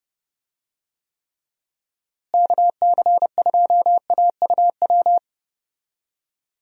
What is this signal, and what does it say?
Morse code, heard as a keyed tone, about 30 wpm, 700 Hz. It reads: KC2AUW
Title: USA Call Signs - 6 Characters - Repeated 2x - 30wpm